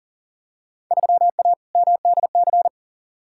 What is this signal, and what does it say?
Morse code, heard as a keyed tone, about 40 wpm, 700 Hz. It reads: 3A MDC
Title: Field Day Exchanges - 40wpm